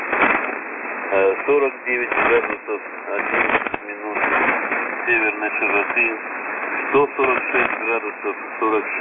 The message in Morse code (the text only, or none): none